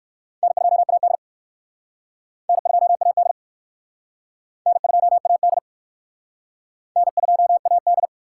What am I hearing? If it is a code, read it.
Morse code, heard as a keyed tone, about 62 wpm, 700 Hz. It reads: N2AD N2AD N2AD N2AD